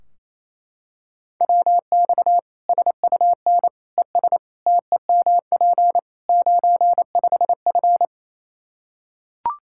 Morse code, heard as a keyed tone, about 28 wpm, 700 Hz. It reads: WX SUN ES TEMP 95F
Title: K1USN SST QSO Practice 28wpm